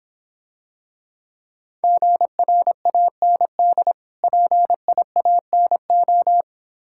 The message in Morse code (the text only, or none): GRAND PIANO